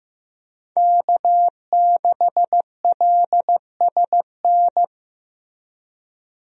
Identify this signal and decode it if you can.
Morse code, heard as a keyed tone, about 15 wpm, 700 Hz. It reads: K6LSN
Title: USA Call Signs - 5 Characters - Repeated 2x - 15wpm